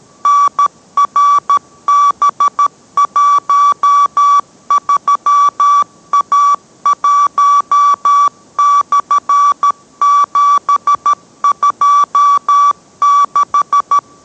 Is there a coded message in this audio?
Yes: Morse code from 0.3 to 14.0 s, heard as a keyed tone, about 13 wpm, 1150 Hz. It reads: NRB13A1/726